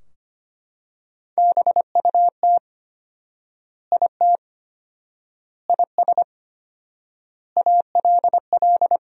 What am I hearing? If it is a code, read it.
Morse code, heard as a keyed tone, about 25 wpm, 700 Hz. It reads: BUT IT IS ALL
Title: 4x Word-Spacing - Sentences from Top 300 Words - 25wpm